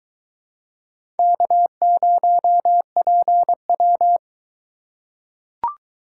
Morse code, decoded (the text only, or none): K0PW